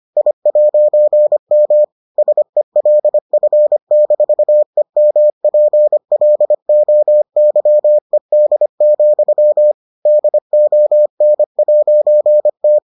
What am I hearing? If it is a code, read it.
Morse code, heard as a keyed tone, about 25 wpm, 600 Hz. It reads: I'M SELF-EMPLOYED, DON'T